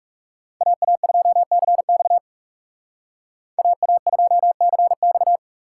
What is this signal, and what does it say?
Morse code, heard as a keyed tone, about 38 wpm, 700 Hz. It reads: AA2CX AA2CX